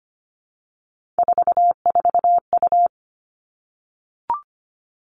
Morse code, heard as a keyed tone, about 25 wpm, 700 Hz. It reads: 44U
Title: Three Challenging Letter-Number Combinations 25wpm